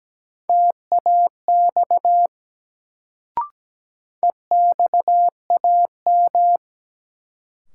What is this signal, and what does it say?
Morse code, heard as a keyed tone, about 17 wpm, 700 Hz. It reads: TAX EXAM
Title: Sentences from Top 1000 Words - Review Part 5 - New Word Drill - 17wpm